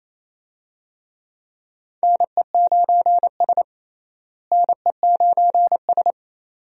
Morse code, heard as a keyed tone, about 28 wpm, 700 Hz. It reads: NE9S NE9S
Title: USA Call Signs - 4 Characters - Repeated 2x - 28wpm